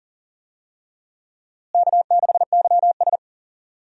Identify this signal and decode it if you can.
Morse code, heard as a keyed tone, about 40 wpm, 700 Hz. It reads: K6YS
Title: USA Call Signs - 4 Characters - Repeated 2x - 40wpm